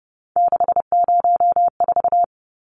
Morse code, heard as a keyed tone, about 30 wpm, 700 Hz. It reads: 604
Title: Numbers - 3-Digit 30wpm